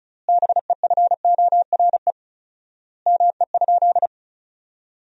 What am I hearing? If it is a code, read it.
Morse code, heard as a keyed tone, about 34 wpm, 700 Hz. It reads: BEFORE ME?
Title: 4x Word-Spacing - Sentences from Top 200 Words - 35wpm